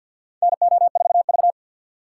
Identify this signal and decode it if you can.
Morse code, heard as a keyed tone, about 47 wpm, 700 Hz. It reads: NO4V
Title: USA Call Signs - 4 Characters - 2x Character Spacing - 50wpm